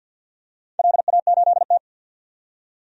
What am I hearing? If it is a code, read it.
Morse code, heard as a keyed tone, about 48 wpm, 700 Hz. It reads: LA8T
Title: DXCC-100 Call Signs - 4 Characters - 50wpm